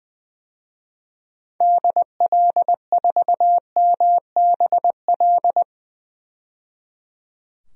Morse code, heard as a keyed tone, about 21 wpm, 700 Hz. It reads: DL4MBL
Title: DXCC-100 Call Signs - 6 Characters - 20wpm